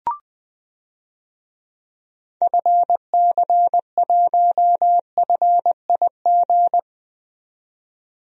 Morse code, heard as a keyed tone, about 21 wpm, 700 Hz. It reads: FC1FIG
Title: DXCC-100 Call Signs - 6 Characters - 20wpm